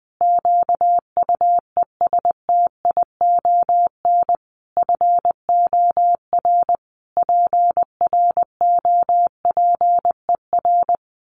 Morse code, heard as a keyed tone, about 20 wpm, 700 Hz. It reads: QUESTION FOR PROPER
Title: Sentences from Top 1500 Words 20wpm